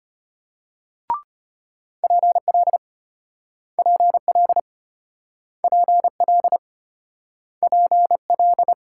Morse code, heard as a keyed tone, about 35 wpm, 700 Hz. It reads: PL PL PL PL